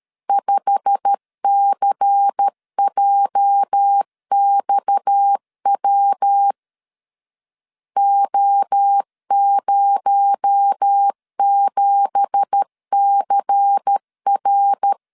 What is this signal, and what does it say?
Morse code, heard as a keyed tone, about 13 wpm, 800 Hz. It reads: 5CJXW O07CR
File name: CW / DET / 1 9 1500C S C 12wpm 3811